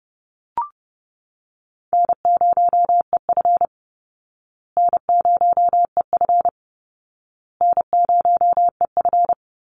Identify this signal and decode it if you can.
Morse code, heard as a keyed tone, about 30 wpm, 700 Hz. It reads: N0EF N0EF N0EF
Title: Easy Call Signs Repeated 3x - 30wpm